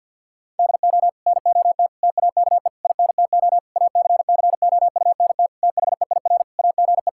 Morse code, heard as a keyed tone, about 50 wpm, 700 Hz. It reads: DO NOT TAKE INTO ACCOUNT THEIR AGE
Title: Sentences from Top 1600 Words 50wpm